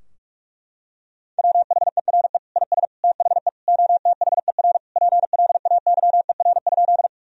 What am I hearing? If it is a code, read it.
Morse code, heard as a keyed tone, about 46 wpm, 700 Hz. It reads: WHERE IS THE OTHER PLAYER?
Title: Sentences from Top 1000 Words - Review Part 4 - 45wpm